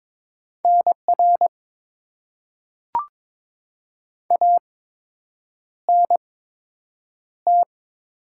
Morse code, heard as a keyed tone, about 22 wpm, spaced as for 8 wpm, 700 Hz. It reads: NR ANT
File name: Top 90 QSO Elements - 8x Character Spacing - 22wpm